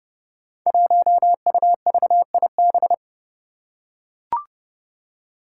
Morse code, heard as a keyed tone, about 31 wpm, 700 Hz. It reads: 1UVIB
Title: Five Challenging Letter-Number Combinations 30wpm